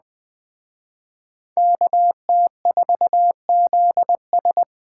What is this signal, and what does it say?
Morse code, heard as a keyed tone, about 21 wpm, 700 Hz. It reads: KT4ZS